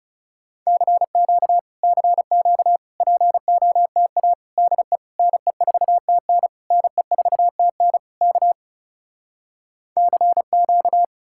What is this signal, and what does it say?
Morse code, heard as a keyed tone, about 37 wpm, 700 Hz. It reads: CQ CQ POTA DE NE4TN NE4TN K CQ
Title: Speed-Racing Common QSO Phrases - 30wpm